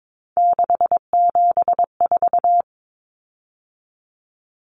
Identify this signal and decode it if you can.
Morse code, heard as a keyed tone, about 22 wpm, 700 Hz. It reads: 674
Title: Common Cut Numbers - 3-Digit 22wpm